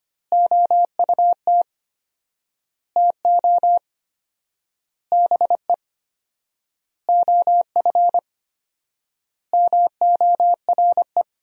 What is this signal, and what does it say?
Morse code, heard as a keyed tone, about 25 wpm, 700 Hz. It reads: OUT TO BE OF MORE
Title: 4x Word-Spacing - Sentences from Top 400 Words - 25wpm